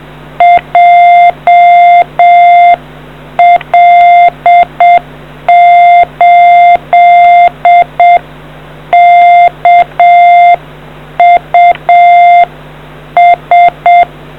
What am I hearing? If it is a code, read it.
Morse code, heard as a keyed tone, about 7 wpm, 700 Hz. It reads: JL8KUS